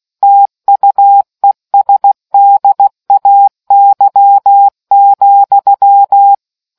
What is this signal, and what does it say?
Morse code, heard as a keyed tone, about 16 wpm, 800 Hz. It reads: TUESDAY,